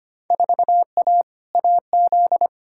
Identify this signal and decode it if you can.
Morse code, heard as a keyed tone, about 25 wpm, 700 Hz. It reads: 4A AZ
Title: Field Day Exchanges Encore - 25wpm